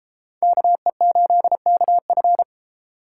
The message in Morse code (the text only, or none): KE8KF